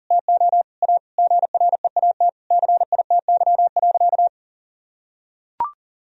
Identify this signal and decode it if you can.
Morse code, heard as a keyed tone, about 40 wpm, 700 Hz. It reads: TO A GREAT CITY.
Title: Sentences from Top 1000 Words - Review Part 1 - 40wpm